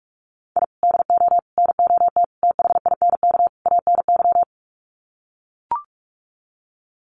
Morse code, heard as a keyed tone, about 44 wpm, 700 Hz. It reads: I DO NOT THINK ANY
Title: Sets of 5 Words Encore 45wpm